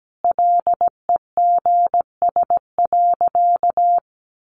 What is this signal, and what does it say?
Morse code, heard as a keyed tone, about 17 wpm, 700 Hz. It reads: LEGS.
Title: A Sound Of Thunder 17wpm